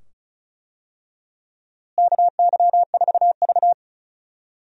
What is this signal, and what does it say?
Morse code, heard as a keyed tone, about 36 wpm, 700 Hz. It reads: KY4V
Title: USA Call Signs - 4 Characters - Repeated 2x - 35wpm